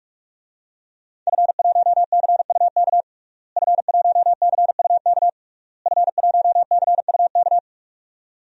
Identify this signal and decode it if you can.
Morse code, heard as a keyed tone, about 46 wpm, 700 Hz. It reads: F1CUK F1CUK F1CUK
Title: DXCC-100 Call Signs - 5 Characters - Repeated 3x - 45wpm